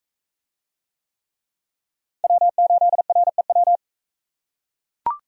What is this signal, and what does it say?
Morse code, heard as a keyed tone, about 40 wpm, 700 Hz. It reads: W8REW